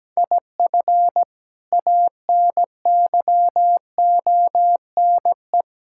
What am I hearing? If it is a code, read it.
Morse code, heard as a keyed tone, about 17 wpm, 700 Hz. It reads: IF ANYONE
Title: Sentences from Top 1000 Words - Review Part 6 - 17wpm